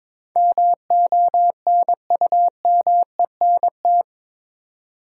Morse code, heard as a keyed tone, about 22 wpm, 700 Hz. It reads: MONUMENT
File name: practice / 22wpm / Eight Letter Words - 22wpm